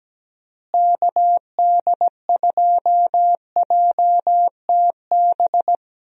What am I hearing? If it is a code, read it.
Morse code, heard as a keyed tone, about 17 wpm, 700 Hz. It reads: KD2JTB
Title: USA Call Signs - 6 Characters - Repeated 3x - 17wpm